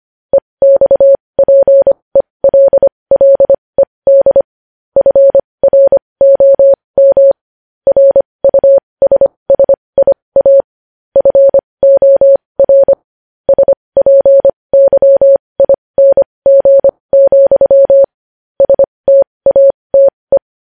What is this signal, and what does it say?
Morse code, heard as a keyed tone, about 25 wpm, spaced as for 20 wpm, 550 Hz. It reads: EXPELLED FROM RUSSIA FOR SPYING, STATE
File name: podcast / News headlines-1x-25x20